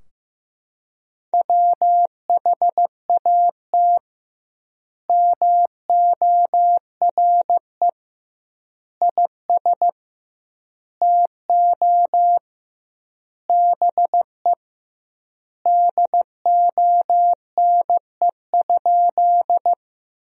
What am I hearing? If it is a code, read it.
Morse code, heard as a keyed tone, about 15 wpm, 700 Hz. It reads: WHAT MORE IS TO BE DONE?
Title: 2x Word-Spacing - Sentences from Top 500 Words - 15wpm